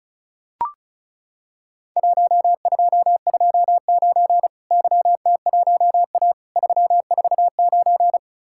Morse code, heard as a keyed tone, about 35 wpm, 700 Hz. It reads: 1229 YT1A 349
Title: Worked All Europe DX Contest - QTC Exchange - 35wpm